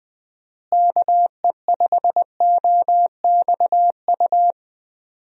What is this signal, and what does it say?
Morse code, heard as a keyed tone, about 20 wpm, 700 Hz. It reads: KE5OXU
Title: USA Call Signs - 6 Characters - Repeated 3x - 20wpm